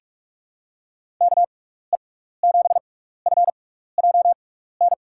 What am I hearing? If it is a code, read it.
Morse code, heard as a keyed tone, about 46 wpm, spaced as for 16 wpm, 700 Hz. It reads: KE7FJN
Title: USA Call Signs - 6 Characters - 6x Character Spacing - 45wpm